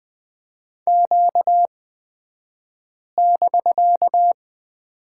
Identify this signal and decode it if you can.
Morse code, heard as a keyed tone, about 20 wpm, spaced as for 5 wpm, 700 Hz. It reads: Q<BK>
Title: 249 Lesson 40d - BK - Two Characters 20wpm